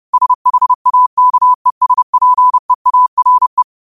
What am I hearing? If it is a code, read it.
Morse code, heard as a keyed tone, about 30 wpm, 1000 Hz. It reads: SHAKESPEARE
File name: qotdfiles / 30wpm